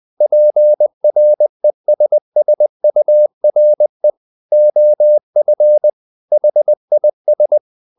Morse code, heard as a keyed tone, about 20 wpm, 600 Hz. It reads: PRESSURE OF HIS
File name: the old man and the sea / the old man and the sea part 1 20wpm